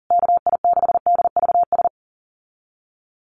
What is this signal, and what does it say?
Morse code, heard as a keyed tone, about 40 wpm, 700 Hz. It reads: KI6DVS
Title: USA Call Signs Repeated 2x 40wpm